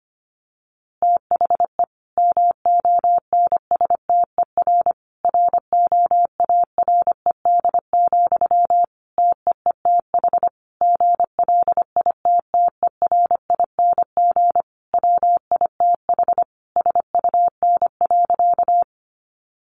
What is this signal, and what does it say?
Morse code, heard as a keyed tone, about 25 wpm, 700 Hz. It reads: THE MONSTER ROARED, TEETH GLITTERING WITH SUN.
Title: A Sound Of Thunder 25wpm